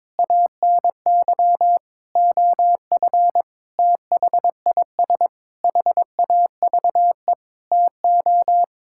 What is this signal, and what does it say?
Morse code, heard as a keyed tone, about 22 wpm, 700 Hz. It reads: ANY OF THIS HAVE TO
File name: Sets of 9 Words - 22wpm